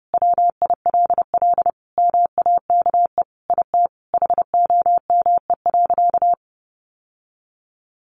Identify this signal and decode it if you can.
Morse code, heard as a keyed tone, about 30 wpm, 700 Hz. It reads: WILL MAKE IT HOME.
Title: Sentences from Top 1500 Words 30wpm